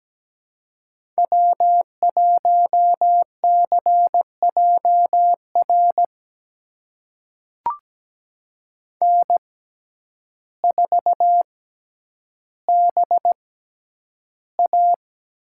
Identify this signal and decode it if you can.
Morse code, heard as a keyed tone, about 17 wpm, spaced as for 9 wpm, 700 Hz. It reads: W1CJR N4BA
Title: USA Call Signs - 6x Character Spacing - 17wpm